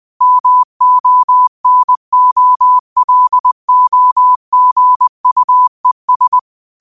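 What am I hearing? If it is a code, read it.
Morse code, heard as a keyed tone, about 20 wpm, 1000 Hz. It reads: MONOLOGUES
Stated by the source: Morse quote practice at 20 wpm.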